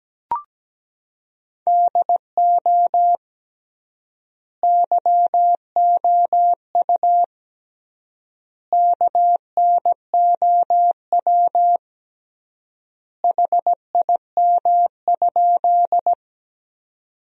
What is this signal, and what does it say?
Morse code, heard as a keyed tone, about 17 wpm, 700 Hz. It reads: DO YOU KNOW HIM?
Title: 3x Word-Spacing - Sentences from Top 100 Words - 17wpm